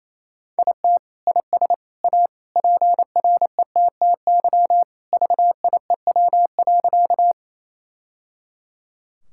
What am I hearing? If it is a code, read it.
Morse code, heard as a keyed tone, about 28 wpm, 700 Hz. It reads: IT IS A PRETTY VIEW.
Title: Sentences from Top 1000 Words - Review Part 6 - 28wpm